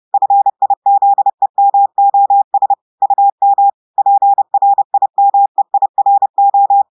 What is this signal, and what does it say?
Morse code, heard as a keyed tone, about 30 wpm, 800 Hz. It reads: FIZEMOS UM PRIMEIRO
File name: audios / portugais / portugais texte en clair 1 vitesse 30 mots minutes